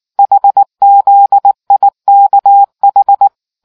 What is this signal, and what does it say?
Morse code, heard as a keyed tone, about 19 wpm, 800 Hz. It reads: HZIKH